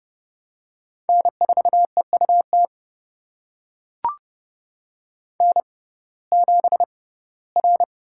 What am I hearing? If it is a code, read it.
Morse code, heard as a keyed tone, about 30 wpm, spaced as for 18 wpm, 700 Hz. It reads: N4EUT N7R
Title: USA Call Signs - 5 Characters - 6x Character Spacing - 30wpm